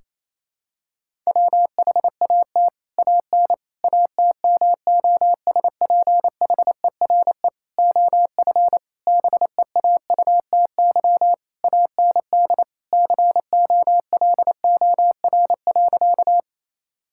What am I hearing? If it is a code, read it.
Morse code, heard as a keyed tone, about 28 wpm, 700 Hz. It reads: WHAT AN ATMOSPHERE OF BEAUTY AND COLOR.
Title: Sentences from Top 1600 Words 28wpm